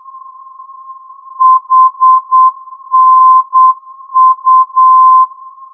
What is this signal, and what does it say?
Morse code, heard as a keyed tone, about 8 wpm, 1040 Hz. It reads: HNU
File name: ndb hanau hnu-432